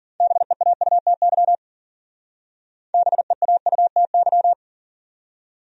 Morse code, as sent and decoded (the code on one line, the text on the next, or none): -... .- ..- - -.-- / -... .- ..- - -.--
BEAUTY BEAUTY